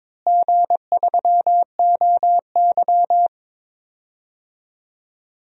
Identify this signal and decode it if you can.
Morse code, heard as a keyed tone, about 22 wpm, 700 Hz. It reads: G3OY
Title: DXCC-100 Call Signs - 4 Characters - Repeated 3x - 22wpm